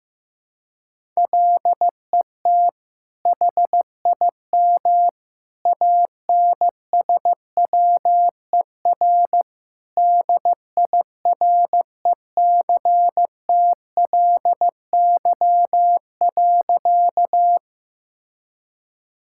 Morse code, heard as a keyed tone, about 15 wpm, 700 Hz. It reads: LET HIM ANSWER DIRECTLY.
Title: Sentences from Top 1000 Words - Review Part 5 - 15wpm